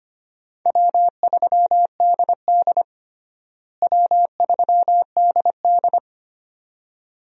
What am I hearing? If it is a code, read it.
Morse code, heard as a keyed tone, about 25 wpm, 700 Hz. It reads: W3DD W3DD
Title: USA Call Signs - 4 Characters - Repeated 2x - 25wpm